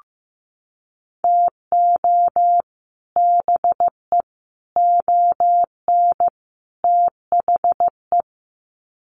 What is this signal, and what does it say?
Morse code, heard as a keyed tone, about 15 wpm, 700 Hz. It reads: TO BE ON THE
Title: Sets of 4 Words 15wpm